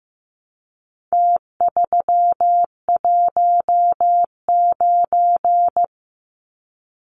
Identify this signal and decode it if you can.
Morse code, heard as a keyed tone, about 15 wpm, 700 Hz. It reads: T319
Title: Numbers - Contest Style - 4-Digit 15wpm